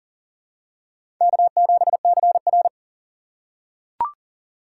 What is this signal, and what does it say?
Morse code, heard as a keyed tone, about 40 wpm, 700 Hz. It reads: K7CR
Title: USA Call Signs - 4 Characters - 2x Character Spacing - 40wpm